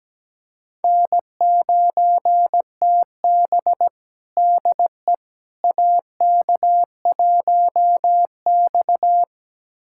Morse code, heard as a keyed tone, about 17 wpm, 700 Hz. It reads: N9TB DE AK1X
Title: USA Callsign DE Callsign - 4 Characters - 17wpm